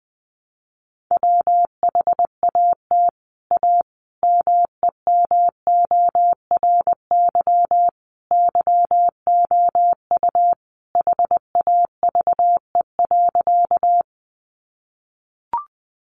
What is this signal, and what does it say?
Morse code, heard as a keyed tone, about 20 wpm, 700 Hz. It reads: WHAT A MEMORY YOU HAVE.
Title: Sentences from Top 1300 Words 20wpm